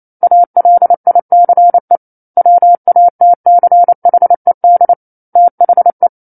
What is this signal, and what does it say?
Morse code, heard as a keyed tone, about 28 wpm, 700 Hz. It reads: ALICE WATCHED THE
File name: 09-Chapter-09 28 WPM